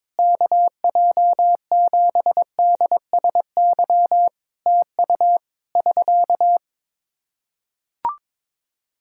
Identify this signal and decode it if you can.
Morse code, heard as a keyed tone, about 22 wpm, 700 Hz. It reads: KJ7DSY TU <SK>
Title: K1USN SST QSO Practice 22wpm